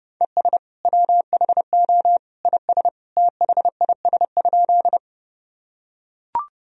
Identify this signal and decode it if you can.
Morse code, heard as a keyed tone, about 32 wpm, 700 Hz. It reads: ES WHO IS THIS?
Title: Sentences from Top 100 Words and Abbreviations 30wpm